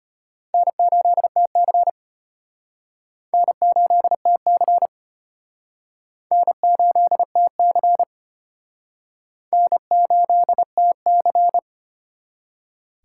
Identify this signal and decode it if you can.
Morse code, heard as a keyed tone, about 33 wpm, 700 Hz. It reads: N8TC N8TC N8TC N8TC